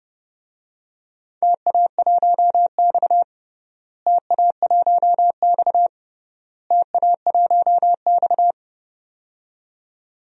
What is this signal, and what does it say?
Morse code, heard as a keyed tone, about 29 wpm, 700 Hz. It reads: TA1X TA1X TA1X
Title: DXCC-100 Call Signs - 4 Characters - Repeated 3x - 30wpm